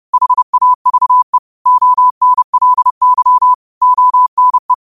Morse code, heard as a keyed tone, about 30 wpm, 1000 Hz. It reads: HAVE ONLY ONE
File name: qotdfiles / 30wpm